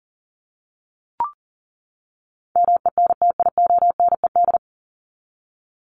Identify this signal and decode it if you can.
Morse code, heard as a keyed tone, about 40 wpm, 700 Hz. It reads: MENTIONED
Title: ED Suffix - Top 100 Words with Base Words 40wpm